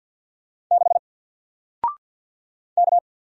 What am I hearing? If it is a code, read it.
Morse code, heard as a keyed tone, about 48 wpm, 700 Hz. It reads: 6 K
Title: Single Letter-Number 50wpm